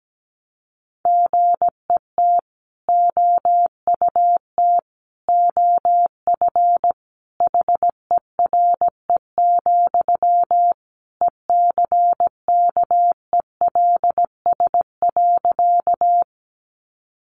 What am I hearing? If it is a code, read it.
Morse code, heard as a keyed tone, about 17 wpm, 700 Hz. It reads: GET OUT OF HERE, ECKELS.
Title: A Sound Of Thunder 17wpm